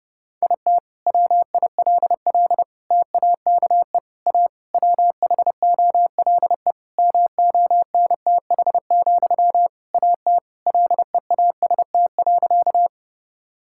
Morse code, heard as a keyed tone, about 31 wpm, 700 Hz. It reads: IT WILL TAKE A WHOLE MONTH, AT LEAST.
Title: Sentences from Top 1000 Words - Review Part 5 - 30wpm